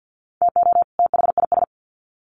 Morse code, heard as a keyed tone, about 50 wpm, 700 Hz. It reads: TO THIS
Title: Sets of 2 Words 50wpm